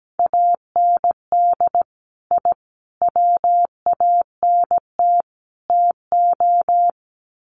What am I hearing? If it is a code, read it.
Morse code, heard as a keyed tone, about 17 wpm, 700 Hz. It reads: AND I WANT TO
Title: Sets of 4 Words 17wpm